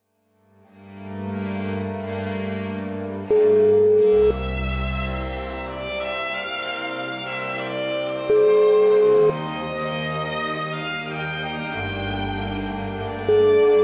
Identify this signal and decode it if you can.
Ringback tone (Europe, 425 Hz), with noise.